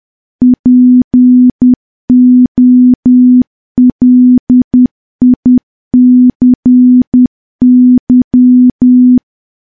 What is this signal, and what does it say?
Morse code, heard as a keyed tone, about 10 wpm, 250 Hz. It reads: POLICY
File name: morse / morse-training-10WPM